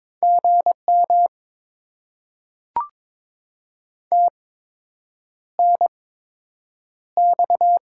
Morse code, heard as a keyed tone, about 22 wpm, spaced as for 8 wpm, 700 Hz. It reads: GM TNX
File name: Top 20 QSO Elements - 8x Character Spacing - 22wpm